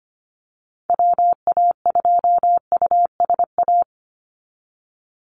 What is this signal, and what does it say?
Morse code, heard as a keyed tone, about 25 wpm, 690 Hz. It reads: WA2USA
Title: NA SOTA Activator Callsigns 25wpm